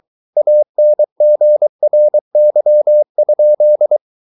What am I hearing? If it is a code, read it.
Morse code, heard as a keyed tone, about 23 wpm, 600 Hz. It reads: ANGRY?